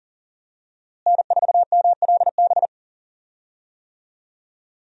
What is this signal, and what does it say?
Morse code, heard as a keyed tone, about 40 wpm, 700 Hz. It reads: N4MLB
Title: USA Call Signs - 5 Characters - Repeated 3x - 40wpm